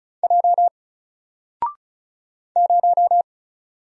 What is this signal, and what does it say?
Morse code, heard as a keyed tone, about 35 wpm, 700 Hz. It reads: J 0
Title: Single Letter-Number Alt 35wpm